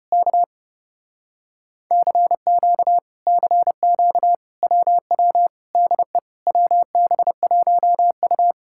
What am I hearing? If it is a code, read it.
Morse code, heard as a keyed tone, about 33 wpm, 700 Hz. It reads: K CQ CQ WW DE WB1U